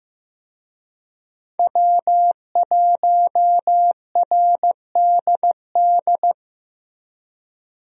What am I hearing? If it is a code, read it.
Morse code, heard as a keyed tone, about 15 wpm, 700 Hz. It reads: W1RDD
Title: USA Call Signs - 5 Characters - Repeated 3x - 15wpm